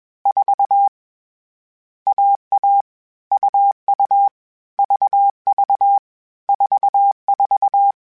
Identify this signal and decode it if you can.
Morse code, heard as a keyed tone, about 21 wpm, 800 Hz. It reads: 4 AA UU VV 44
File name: Short Runs / Short Runs-mp3s / AUV AUV4 @20